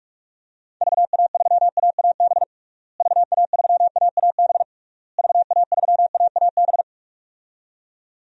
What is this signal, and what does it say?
Morse code, heard as a keyed tone, about 46 wpm, 700 Hz. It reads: VA3AAB VA3AAB VA3AAB
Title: DXCC-100 Call Signs - 6 Characters - Repeated 3x - 45wpm